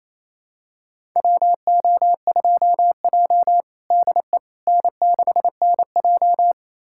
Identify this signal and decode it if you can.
Morse code, heard as a keyed tone, about 28 wpm, 700 Hz. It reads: WO2J DE N6NJ
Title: USA Callsign DE Callsign - 4 Characters - 28wpm